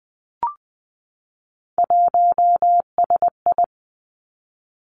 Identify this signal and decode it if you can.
Morse code, heard as a keyed tone, about 20 wpm, 700 Hz. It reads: 1SI
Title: Three Challenging Letter-Number Combinations 20wpm